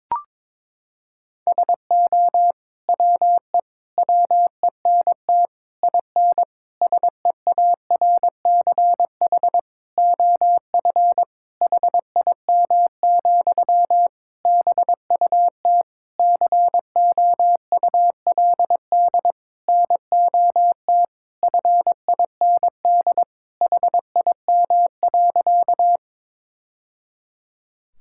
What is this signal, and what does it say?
Morse code, heard as a keyed tone, about 23 wpm, 700 Hz. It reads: SO WE WENT IN SEARCH OF HIM, BUT COULD NOT FIND HIM.
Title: Sentences from Top 1000 Words - Review Part 5 - 22wpm